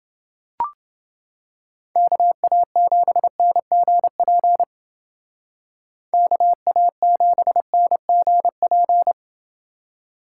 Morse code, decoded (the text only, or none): KA7NGP KA7NGP